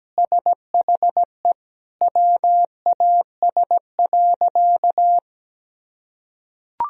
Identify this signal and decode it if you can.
Morse code, heard as a keyed tone, about 17 wpm, 700 Hz. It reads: SHE WAS.
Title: Sentences from Top 1000 Words - Review Part 2 - 17wpm